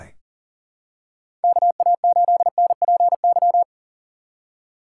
Morse code, heard as a keyed tone, about 40 wpm, 700 Hz. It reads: KA8NPY
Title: USA Call Signs - 6 Characters - 6x Character Spacing - 40wpm